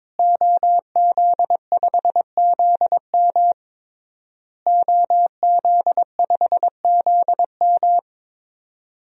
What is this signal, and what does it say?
Morse code, heard as a keyed tone, about 22 wpm, 700 Hz. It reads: OZ5ZM OZ5ZM
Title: DXCC-100 Call Signs - 5 Characters - Repeated 3x - 22wpm